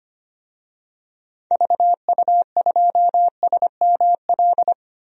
Morse code, heard as a keyed tone, about 25 wpm, 700 Hz. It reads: VU2SML
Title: DXCC-100 Call Signs - 6 Characters - 25wpm